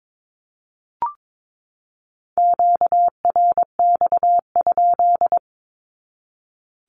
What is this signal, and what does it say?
Morse code, heard as a keyed tone, about 22 wpm, 700 Hz. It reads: QRX?
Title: Q-Codes 22wpm